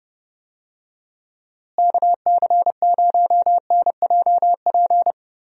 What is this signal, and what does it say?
Morse code, heard as a keyed tone, about 31 wpm, 700 Hz. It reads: KC0NJP
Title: USA Call Signs - 6 Characters - Repeated 2x - 30wpm